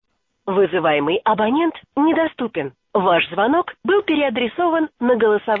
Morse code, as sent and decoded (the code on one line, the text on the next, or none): none